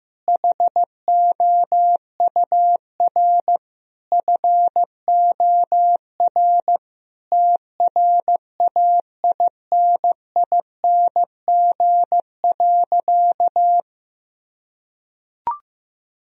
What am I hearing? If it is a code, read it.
Morse code, heard as a keyed tone, about 16 wpm, 700 Hz. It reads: HOUR FOR TRAINING.
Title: Sentences from Top 1000 Words - Review Part 4 - 15wpm